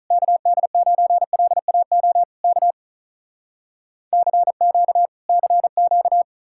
Morse code, heard as a keyed tone, about 46 wpm, 700 Hz. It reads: KD9LAO K CQ CQ